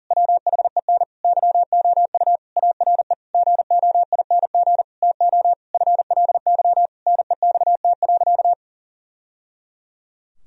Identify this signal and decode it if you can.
Morse code, heard as a keyed tone, about 40 wpm, 700 Hz. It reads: WHEN YOU ARE GOING TO FLY NEXT.
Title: Sentences from Top 1000 Words - Review Part 2 - 40wpm